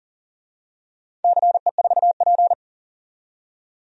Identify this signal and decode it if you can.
Morse code, heard as a keyed tone, about 40 wpm, 700 Hz. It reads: CE4P
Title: DXCC-100 Call Signs - 4 Characters - 40wpm